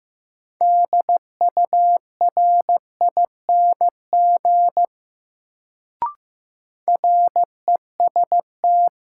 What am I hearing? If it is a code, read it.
Morse code, heard as a keyed tone, about 15 wpm, 700 Hz. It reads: DURING REST
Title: Sets of 2 Words - New Word Drill - 15wpm